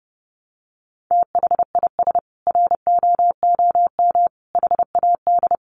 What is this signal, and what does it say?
Morse code, heard as a keyed tone, about 30 wpm, 700 Hz. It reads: THIS ROOM HAD
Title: Sentences from Top 700 Words 30wpm